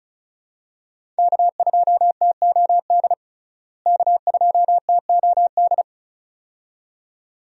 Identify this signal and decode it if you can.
Morse code, heard as a keyed tone, about 34 wpm, 700 Hz. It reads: K2TOD K2TOD
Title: USA Call Signs - 5 Characters - Repeated 2x - 35wpm